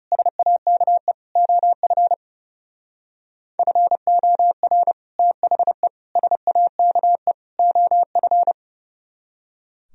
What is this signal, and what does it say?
Morse code, heard as a keyed tone, about 35 wpm, 700 Hz. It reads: SAKE OF FOR THE SAKE OF
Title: Speed-Racing - Sets of 4 Words 30wpm